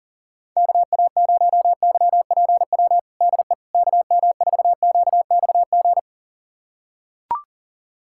Morse code, heard as a keyed tone, about 37 wpm, 700 Hz. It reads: KA0YPW DE KM4QXG
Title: USA Callsign DE Callsign - 6 Characters - 40wpm